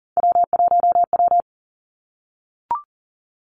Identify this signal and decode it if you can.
Morse code, heard as a keyed tone, about 42 wpm, 700 Hz. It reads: W1W
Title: Three Challenging Letter-Number Combinations 40wpm